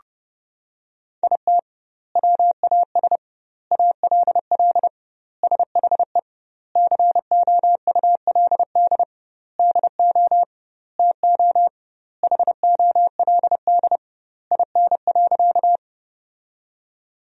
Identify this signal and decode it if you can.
Morse code, heard as a keyed tone, about 30 wpm, 700 Hz. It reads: IT WAS ALL SHE COULD DO TO HOLD IN.
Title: 2x Word-Spacing - Sentences from Top 400 Words - 30wpm